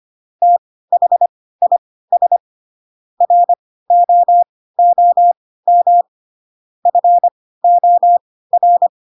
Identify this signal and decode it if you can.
Morse code, heard as a keyed tone, about 25 wpm, spaced as for 16 wpm, 700 Hz. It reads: THIS ROOM FOR
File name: Pinocchio / Pinocchio ch4 10 25 ik1hge